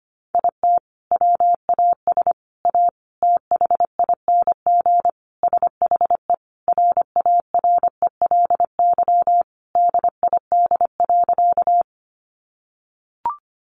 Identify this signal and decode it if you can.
Morse code, heard as a keyed tone, about 25 wpm, 700 Hz. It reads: IT WAS A THING SHE RARELY DID.
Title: Sentences from Top 1400 Words 25wpm